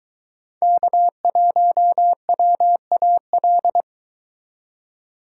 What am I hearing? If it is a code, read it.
Morse code, heard as a keyed tone, about 23 wpm, 700 Hz. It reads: K1WAL